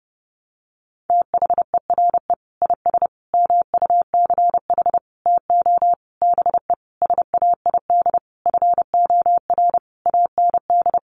Morse code, heard as a keyed tone, about 30 wpm, 700 Hz. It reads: THERE IS MUCH TO BE SAID FOR AND
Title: Sentences from Top 1300 Words 30wpm